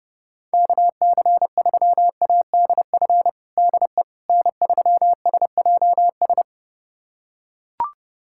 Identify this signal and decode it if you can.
Morse code, heard as a keyed tone, about 30 wpm, 700 Hz. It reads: KC3ADF DE N3SJS
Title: USA Callsign DE Callsign 30wpm